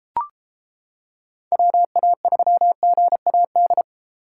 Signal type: Morse code

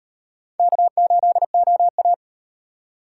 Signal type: Morse code

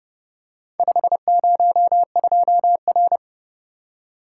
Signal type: Morse code

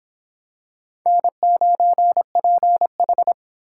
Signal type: Morse code

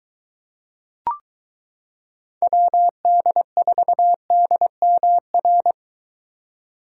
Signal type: Morse code

